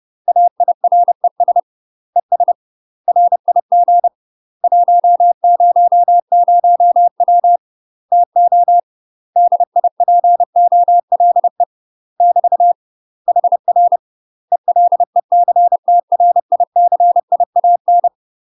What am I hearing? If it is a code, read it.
Morse code, heard as a keyed tone, about 30 wpm, 700 Hz. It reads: AIRES ES RIG 100W TO DIPOLE <BT> HR ELECTRICIAN
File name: UK MASTER QSOs TRAINING / 30WPM / UK MASTER QSOs 0022